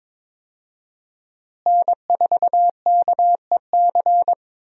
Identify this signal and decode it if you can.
Morse code, heard as a keyed tone, about 22 wpm, 700 Hz. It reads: N4KEC
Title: USA Call Signs - 5 Characters - Repeated 3x - 22wpm